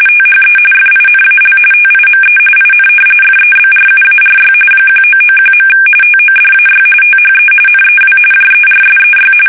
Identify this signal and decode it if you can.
RTTY, about 45.45 baud. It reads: THE FINAL TIME IN FEBRUARY 1991. TODAY THE NEW JERSEY
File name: fsk850 / battleshipnj